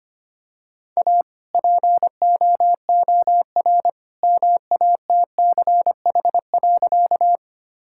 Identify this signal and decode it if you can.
Morse code, heard as a keyed tone, about 25 wpm, 700 Hz. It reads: A POOR MATCH.
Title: Sentences from Top 1600 Words 25wpm